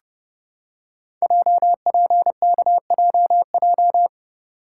Morse code, heard as a keyed tone, about 30 wpm, 700 Hz. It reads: JPKJJ
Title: Code Groups - 5 Letter:Numbers - 30wpm